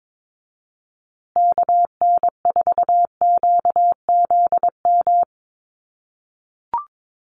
Morse code, heard as a keyed tone, about 22 wpm, 700 Hz. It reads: KN4QZM
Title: USA Call Signs Repeated 3x 22wpm